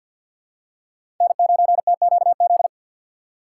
Morse code, heard as a keyed tone, about 52 wpm, 700 Hz. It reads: N9TQZ
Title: USA Call Signs - 5 Characters - 2x Character Spacing - 50wpm